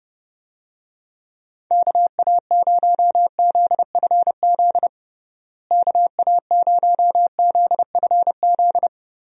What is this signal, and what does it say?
Morse code, heard as a keyed tone, about 30 wpm, 700 Hz. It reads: KA0ZFZ KA0ZFZ
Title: USA Call Signs - 6 Characters - Repeated 2x - 30wpm